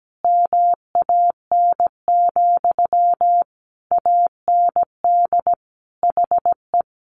Morse code, heard as a keyed tone, about 17 wpm, 700 Hz. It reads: MAN, AND HE
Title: Sentences from Top 1500 Words 17wpm